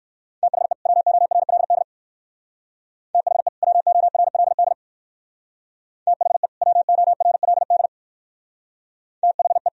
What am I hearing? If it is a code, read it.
Morse code, heard as a keyed tone, about 57 wpm, 700 Hz. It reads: THE WORLD THE WORLD THE WORLD THE